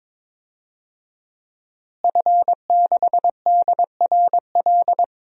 Morse code, heard as a keyed tone, about 21 wpm, 700 Hz. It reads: F6DRL
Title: DXCC-100 Call Signs - 5 Characters - Repeated 3x - 22wpm